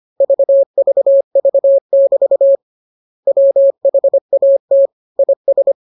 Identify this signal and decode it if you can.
Morse code, heard as a keyed tone, about 25 wpm, 550 Hz. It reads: VVV<BT> WHAT IS
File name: jokes / joke 15 25 0000